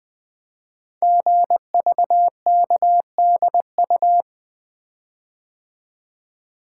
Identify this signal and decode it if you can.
Morse code, heard as a keyed tone, about 20 wpm, 700 Hz. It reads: GVKDU
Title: Code Groups - 5 Letter:Numbers - 20wpm